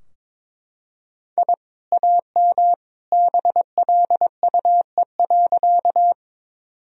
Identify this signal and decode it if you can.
Morse code, heard as a keyed tone, about 22 wpm, 700 Hz. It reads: I AM BLUE.
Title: Sentences from Top 1000 Words - Review Part 3 - 22wpm